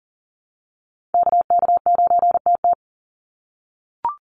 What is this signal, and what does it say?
Morse code, heard as a keyed tone, about 40 wpm, 700 Hz. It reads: KK9TT
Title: USA Call Signs 40wpm